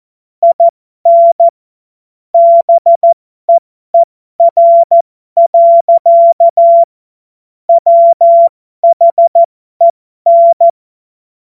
Morse code, heard as a keyed tone, about 14 wpm, spaced as for 12 wpm, 680 Hz. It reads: IN BEER. WHEN
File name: Morsekurs / Witze